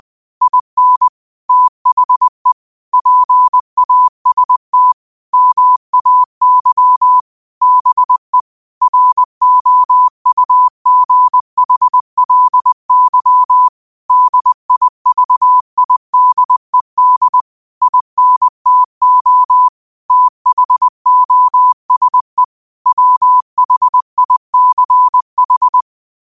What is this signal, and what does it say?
Morse code, heard as a keyed tone, about 20 wpm, 1000 Hz. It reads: IN THE PAST MAY BE ROUGHLY DIVIDED INTO THOSE WHICH
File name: qotdfiles / 20wpm